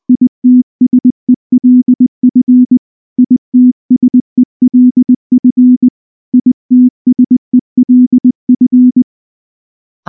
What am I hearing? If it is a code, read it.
Morse code, heard as a keyed tone, about 19 wpm, 250 Hz. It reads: ITSELF ITSELF ITSELF
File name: morse / morse-training-20WPM-7